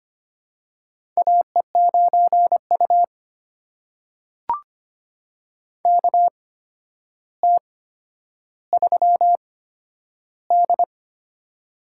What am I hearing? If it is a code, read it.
Morse code, heard as a keyed tone, about 25 wpm, spaced as for 11 wpm, 700 Hz. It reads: AE9U KT3D
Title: USA Call Signs - 4 Characters - 8x Character Spacing - 25wpm